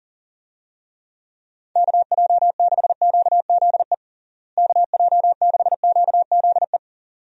Morse code, heard as a keyed tone, about 40 wpm, 700 Hz. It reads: KJ6QZE KJ6QZE
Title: USA Call Signs - 6 Characters - Repeated 2x - 40wpm